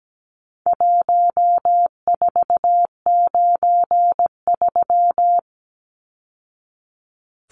Morse code, heard as a keyed tone, about 17 wpm, 700 Hz. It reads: 1493
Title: Numbers - Contest Style - 4-Digit 17wpm